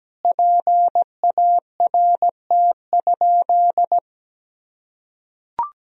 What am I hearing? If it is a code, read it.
Morse code, heard as a keyed tone, about 17 wpm, 700 Hz. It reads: PART?
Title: Sentences from Top 1000 Words - Review Part 1 - 17wpm